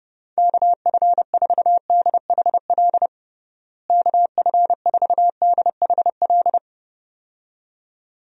Morse code, heard as a keyed tone, about 31 wpm, 700 Hz. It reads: KF4DHL KF4DHL
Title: USA Call Signs - 6 Characters - Repeated 2x - 30wpm